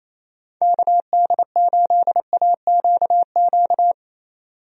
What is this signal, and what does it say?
Morse code, heard as a keyed tone, about 28 wpm, 700 Hz. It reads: KD8AQQ
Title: USA Call Signs - 6 Characters - 28wpm